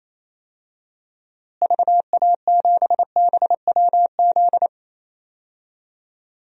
Morse code, heard as a keyed tone, about 27 wpm, 700 Hz. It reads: VA7BWZ
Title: DXCC-100 Call Signs - 6 Characters - Repeated 2x - 28wpm